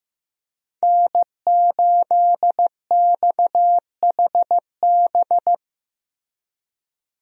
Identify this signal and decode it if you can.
Morse code, heard as a keyed tone, about 16 wpm, 700 Hz. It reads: N8XHB
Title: USA Call Signs - 5 Characters - Repeated 2x - 15wpm